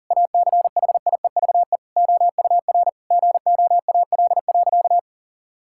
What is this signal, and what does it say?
Morse code, heard as a keyed tone, about 40 wpm, 700 Hz. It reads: ACHIEVE OUR GOAL.
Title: Sentences from Top 1000 Words - Review Part 6 - 40wpm